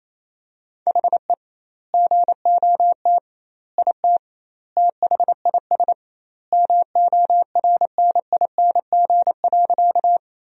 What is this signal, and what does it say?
Morse code, heard as a keyed tone, about 28 wpm, 700 Hz. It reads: HE GOT IT THIS MORNING.
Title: 2x Word-Spacing - Sentences from Top 400 Words - 28wpm